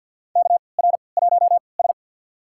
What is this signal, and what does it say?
Morse code, heard as a keyed tone, about 51 wpm, spaced as for 28 wpm, 700 Hz. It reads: KR1S
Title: USA Call Signs - 4 Characters - 3x Character Spacing - 50wpm